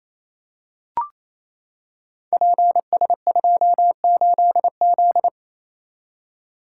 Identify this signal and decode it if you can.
Morse code, heard as a keyed tone, about 28 wpm, 700 Hz. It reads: PS28Z
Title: Code Groups - 5 Letter:Numbers - 28wpm